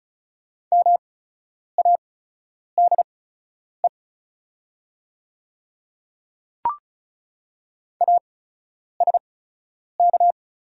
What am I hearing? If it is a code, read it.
Morse code, heard as a keyed tone, about 35 wpm, spaced as for 10 wpm, 690 Hz. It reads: MADE ASK
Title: 8x Character Spacing - Top 200 Words - 35wpm